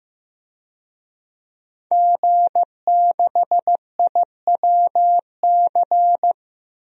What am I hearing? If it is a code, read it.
Morse code, heard as a keyed tone, about 15 wpm, 700 Hz. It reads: G6IWC
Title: DXCC-100 Call Signs - 5 Characters - Repeated 3x - 15wpm